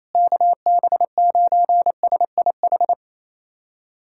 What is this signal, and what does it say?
Morse code, heard as a keyed tone, about 28 wpm, 700 Hz. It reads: KB9SIH
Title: USA Callsign DE Callsign 28wpm